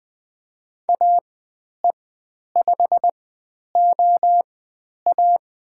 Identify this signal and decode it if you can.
Morse code, heard as a keyed tone, about 20 wpm, spaced as for 10 wpm, 700 Hz. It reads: AE5OA
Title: 109 Lesson 18g - Letter m - Call Signs 20wpm